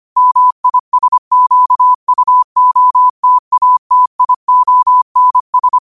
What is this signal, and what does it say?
Morse code, heard as a keyed tone, about 25 wpm, 1000 Hz. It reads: MISQUOTATIONS